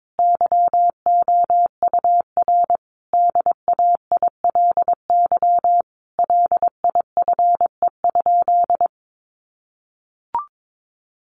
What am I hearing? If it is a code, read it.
Morse code, heard as a keyed tone, about 22 wpm, 700 Hz. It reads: YOUR DAILY LIFE?
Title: Sentences from Top 1100 Words 22wpm